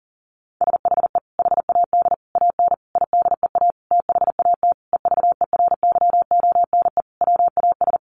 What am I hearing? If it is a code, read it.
Morse code, heard as a keyed tone, about 40 wpm, 700 Hz. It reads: SHE HAD AN IDEA THAT EVERYONE WAS
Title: Sentences from Top 1400 Words 40wpm